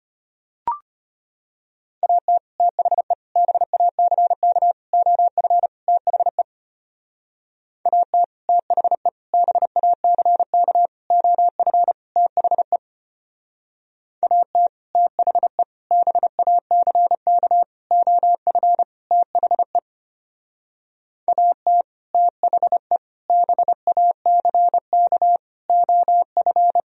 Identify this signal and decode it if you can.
Morse code, heard as a keyed tone, about 32 wpm, 700 Hz. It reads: AT THE BACK OF THE AT THE BACK OF THE AT THE BACK OF THE AT THE BACK OF